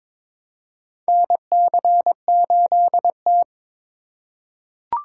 Morse code, heard as a keyed tone, about 21 wpm, 700 Hz. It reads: NC8T
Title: USA Call Signs - 4x Character Spacing - 22wpm